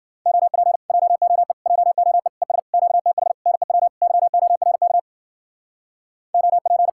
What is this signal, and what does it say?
Morse code, heard as a keyed tone, about 54 wpm, 700 Hz. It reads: OP JOE JOE ES QTH NEW YORK OP